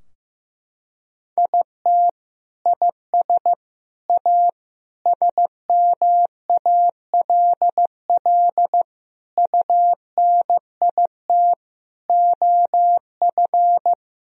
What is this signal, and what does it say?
Morse code, heard as a keyed tone, about 15 wpm, 700 Hz. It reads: IT IS A SMALL UNIT OF
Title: Sentences from Top 1000 Words - Review Part 2 - 15wpm